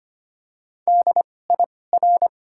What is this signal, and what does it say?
Morse code, heard as a keyed tone, about 25 wpm, spaced as for 18 wpm, 700 Hz. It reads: DIR
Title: 2x Character Spacing - Top 500 Words - 25wpm